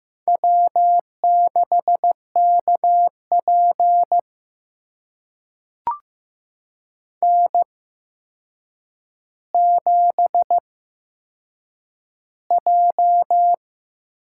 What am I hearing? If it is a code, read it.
Morse code, heard as a keyed tone, about 15 wpm, spaced as for 7 wpm, 700 Hz. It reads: W6KP N7J
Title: USA Call Signs - 4 Characters - 8x Character Spacing - 15wpm